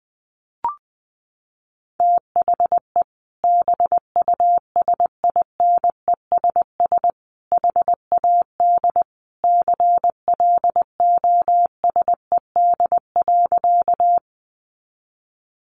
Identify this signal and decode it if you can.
Morse code, heard as a keyed tone, about 20 wpm, 700 Hz. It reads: THE BUSINESS HAD CLOSED.
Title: Sentences from Top 1400 Words 20wpm